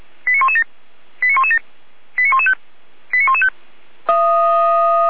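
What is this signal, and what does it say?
Five-tone selective call (ZVEI), heard as quick runs of five tones, tones of about 70 ms. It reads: 89117 89117 89115 89115
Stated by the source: Five-tone ZVEI selective call, address 89501, received on an SDR.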